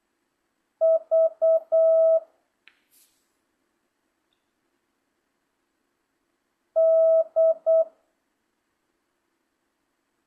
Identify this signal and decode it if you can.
Morse code, heard as a keyed tone, about 8 wpm, 650 Hz. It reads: V D